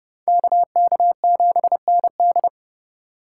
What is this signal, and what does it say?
Morse code, heard as a keyed tone, about 30 wpm, 700 Hz. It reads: KK7ND